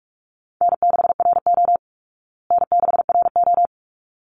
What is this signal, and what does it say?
Morse code, heard as a keyed tone, about 44 wpm, 700 Hz. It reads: N6RO N6RO
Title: Call Signs Repeated 3x - 45wpm